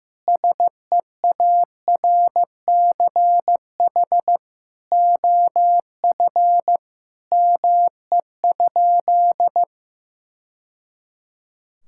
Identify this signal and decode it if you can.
Morse code, heard as a keyed tone, about 16 wpm, 700 Hz. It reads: SEARCH OF ME?
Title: Sentences from Top 1000 Words - Review Part 5 - 15wpm